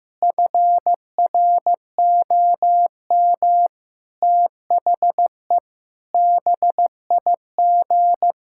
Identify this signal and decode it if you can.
Morse code, heard as a keyed tone, about 15 wpm, 700 Hz. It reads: FROM THE BIG
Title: Pangram Challenge v2 - 15wpm